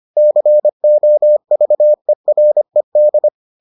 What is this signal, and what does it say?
Morse code, heard as a keyed tone, about 25 wpm, 600 Hz. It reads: COVERED